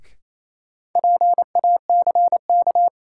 Morse code, heard as a keyed tone, about 27 wpm, 700 Hz. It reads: PACK
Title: Sentences from Top 1300 Words - New Word Drill Alt - 28wpm